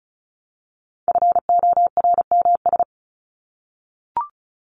Morse code, heard as a keyed tone, about 34 wpm, 700 Hz. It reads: FORMS
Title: ES:S Suffix - Top 100 Words with Base Words 35wpm